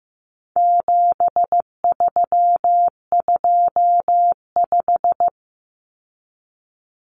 Morse code, heard as a keyed tone, about 15 wpm, 700 Hz. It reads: 7325
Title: Numbers - 4-Digit 15wpm